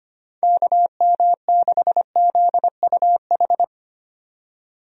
Morse code, heard as a keyed tone, about 25 wpm, 700 Hz. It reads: KM6ZUH
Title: USA Call Signs - 6 Characters - 25wpm